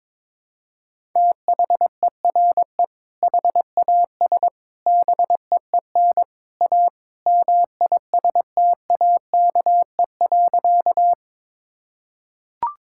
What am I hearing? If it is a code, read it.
Morse code, heard as a keyed tone, about 22 wpm, 700 Hz. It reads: THERE HAS BEEN A MISTAKE.
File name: practice / 22wpm / Sentences with 3-Word Phrases - 22wpm